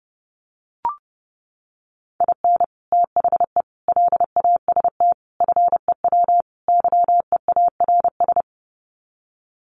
Morse code, heard as a keyed tone, about 28 wpm, 700 Hz. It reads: IN THE LAST FEW YEARS
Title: Sets of 5 Words 30wpm